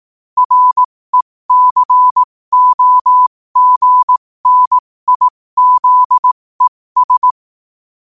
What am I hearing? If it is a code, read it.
Morse code, heard as a keyed tone, about 18 wpm, spaced as for 15 wpm, 1000 Hz. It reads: RECOGNIZES
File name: qotdfiles / 15wpm